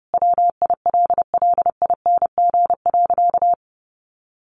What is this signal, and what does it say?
Morse code, heard as a keyed tone, about 30 wpm, 700 Hz. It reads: WILLING.
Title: Sentences from Top 1400 Words 30wpm